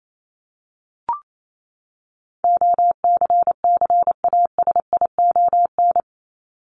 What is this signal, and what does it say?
Morse code, heard as a keyed tone, about 28 wpm, 700 Hz. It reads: OCCASION
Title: Top 1501-1750 Words 28wpm